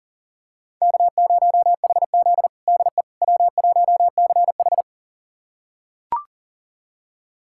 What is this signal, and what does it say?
Morse code, heard as a keyed tone, about 37 wpm, 700 Hz. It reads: K0HZ DE W1CH
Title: USA Callsign DE Callsign - 4 Characters - 40wpm